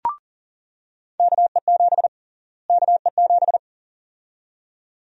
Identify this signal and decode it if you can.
Morse code, heard as a keyed tone, about 40 wpm, 700 Hz. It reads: KE7 KE7
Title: USA Call Sign Prefixes - Repeated 2x - 40wpm